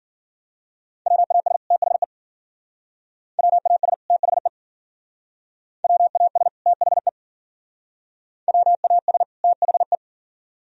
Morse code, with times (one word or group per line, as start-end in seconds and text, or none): 1.1-1.6 WAS
1.7-2.0 THE
3.4-3.9 WAS
4.1-4.5 THE
5.8-6.5 WAS
6.7-7.1 THE
8.5-9.2 WAS
9.4-9.9 THE